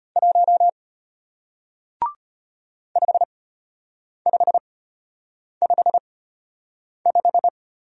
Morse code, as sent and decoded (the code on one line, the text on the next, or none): .---- / ..... ..... ..... .....
1 5555